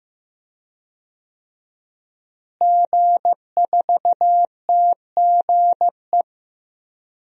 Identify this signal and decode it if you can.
Morse code, heard as a keyed tone, about 15 wpm, 700 Hz. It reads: G4TGE
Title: DXCC-100 Call Signs - 5 Characters - Repeated 3x - 15wpm